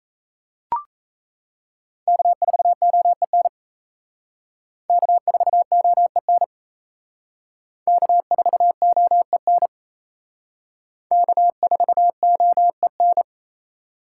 Morse code, heard as a keyed tone, about 34 wpm, 700 Hz. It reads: K4OEN K4OEN K4OEN K4OEN
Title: Speed Racing - USA 1x3 Call Signs 28wpm